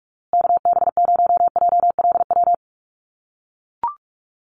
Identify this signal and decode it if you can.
Morse code, heard as a keyed tone, about 44 wpm, 700 Hz. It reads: KB0JLW
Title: USA Call Signs Repeated 2x 45wpm